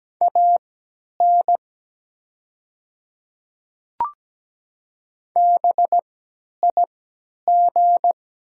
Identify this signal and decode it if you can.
Morse code, heard as a keyed tone, about 17 wpm, spaced as for 10 wpm, 700 Hz. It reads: AN BIG
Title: 3x Character Spacing - Top 200 Words - 17wpm